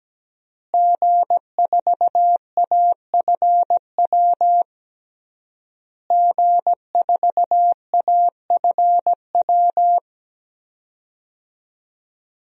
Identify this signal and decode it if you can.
Morse code, heard as a keyed tone, about 17 wpm, 700 Hz. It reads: G4AFW G4AFW
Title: DXCC-100 Call Signs - 5 Characters - Repeated 3x - 17wpm